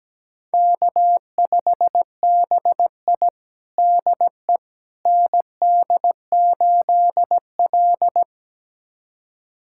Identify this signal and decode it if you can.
Morse code, heard as a keyed tone, about 17 wpm, 700 Hz. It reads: K5BI DE ND8L
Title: USA Callsign DE Callsign - 4 Characters - 17wpm